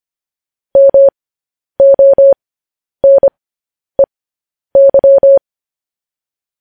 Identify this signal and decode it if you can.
Morse code, heard as a keyed tone, about 25 wpm, spaced as for 10 wpm, 550 Hz. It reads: MONEY
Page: news headlines in morse code at 10 wpm